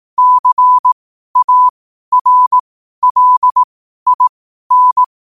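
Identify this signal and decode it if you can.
Morse code, heard as a keyed tone, about 18 wpm, spaced as for 12 wpm, 1000 Hz. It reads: CARLIN
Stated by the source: Morse code at 12 wpm.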